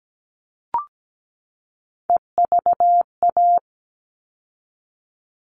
Morse code, heard as a keyed tone, about 17 wpm, 700 Hz. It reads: EVA
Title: All Cut Numbers - 3-Digit 17wpm